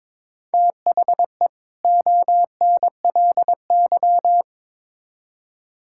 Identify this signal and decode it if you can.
Morse code, heard as a keyed tone, about 22 wpm, 700 Hz. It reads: THE ONLY
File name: Build A Phrase - 2-Word Phrases - 22wpm